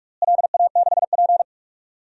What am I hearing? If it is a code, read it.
Morse code, heard as a keyed tone, about 42 wpm, 700 Hz. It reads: LA6P
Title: DXCC-100 Call Signs - 4 Characters - Repeated 2x - 45wpm